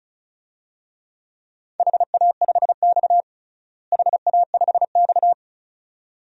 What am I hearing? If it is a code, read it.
Morse code, heard as a keyed tone, about 36 wpm, 700 Hz. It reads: HA5X HA5X
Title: DXCC-100 Call Signs - 4 Characters - Repeated 2x - 35wpm